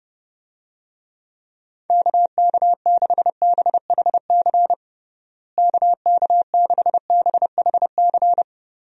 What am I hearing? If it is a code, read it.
Morse code, heard as a keyed tone, about 30 wpm, 700 Hz. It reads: KK6BHC KK6BHC
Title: USA Call Signs - 6 Characters - Repeated 2x - 30wpm